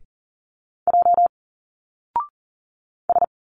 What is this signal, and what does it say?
Morse code, heard as a keyed tone, about 39 wpm, 700 Hz. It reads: J S
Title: Single Letters 40wpm